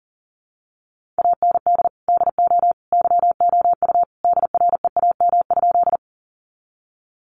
Morse code, heard as a keyed tone, about 37 wpm, 700 Hz. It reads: AND DO YOU DREAM?
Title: Sentences from Top 1500 Words 40wpm